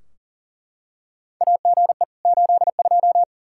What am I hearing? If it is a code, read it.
Morse code, heard as a keyed tone, about 40 wpm, 690 Hz. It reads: AGE 82
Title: Practice QSO - Exchange 3 - More Info - 40wpm